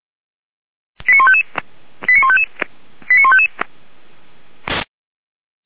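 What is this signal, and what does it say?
Five-tone selective call (ZVEI), heard as quick runs of five tones, tones of about 70 ms. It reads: 89155 89155 89155
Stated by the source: ZVEI five-tone call 89155 received by SDR.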